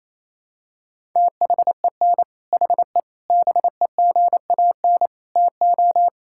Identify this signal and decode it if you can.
Morse code, heard as a keyed tone, about 28 wpm, 700 Hz. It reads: THEN HE BEGAN TO
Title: Sentences from Top 1000 Words - Review Part 5 - 28wpm